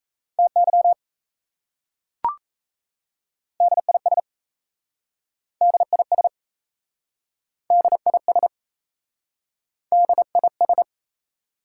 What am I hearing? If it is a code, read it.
Morse code, heard as a keyed tone, about 38 wpm, 700 Hz. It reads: TY DIS DIS DIS DIS